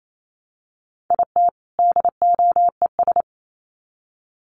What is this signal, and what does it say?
Morse code, heard as a keyed tone, about 28 wpm, 700 Hz. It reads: IT DOES
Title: Sets of 2 Words 28wpm